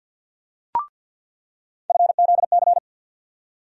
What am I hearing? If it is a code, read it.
Morse code, heard as a keyed tone, about 48 wpm, 700 Hz. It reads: FZC
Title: USA Call Sign Postfixes - 3 Letters 50wpm